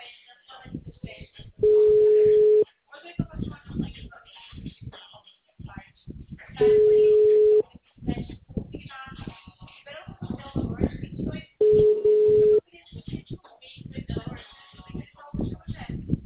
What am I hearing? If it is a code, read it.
Ringback tone (Europe, 425 Hz), with speech.